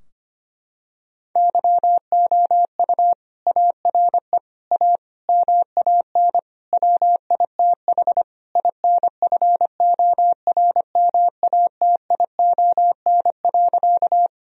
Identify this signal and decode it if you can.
Morse code, heard as a keyed tone, about 25 wpm, 700 Hz. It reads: YOU ARE A MAN WITH INFORMATION.
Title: Sentences from Top 1000 Words - Review Part 3 - 25wpm